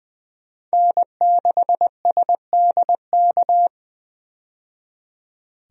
Morse code, heard as a keyed tone, about 20 wpm, 700 Hz. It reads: N6SDK
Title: USA Call Signs - 5 Characters - 20wpm